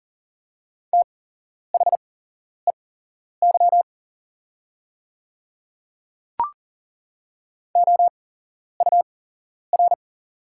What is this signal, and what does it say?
Morse code, heard as a keyed tone, about 40 wpm, spaced as for 11 wpm, 700 Hz. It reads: THEY OUR
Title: 8x Character Spacing - Top 200 Words - 40wpm